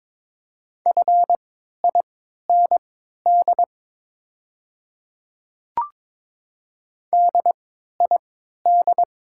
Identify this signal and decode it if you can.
Morse code, heard as a keyed tone, about 22 wpm, spaced as for 12 wpm, 700 Hz. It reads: FIND DID
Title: 3x Character Spacing - Top 100 Words - 22wpm